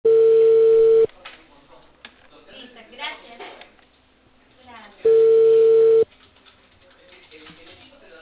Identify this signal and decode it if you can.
Ringback tone (China, 450 Hz), with speech.